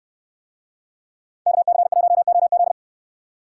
Morse code, heard as a keyed tone, about 64 wpm, 690 Hz. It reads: KC1OZ